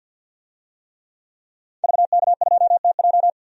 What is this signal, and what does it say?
Morse code, heard as a keyed tone, about 52 wpm, 700 Hz. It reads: VK2TJ
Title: DXCC-100 Call Signs - 5 Characters - Repeated 2x - 50wpm